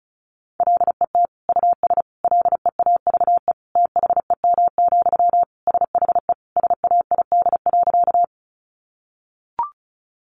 Morse code, heard as a keyed tone, about 34 wpm, 700 Hz. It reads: LET US LEAVE THEM, SHE SAID.
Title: Sentences from Top 400 Words 35wpm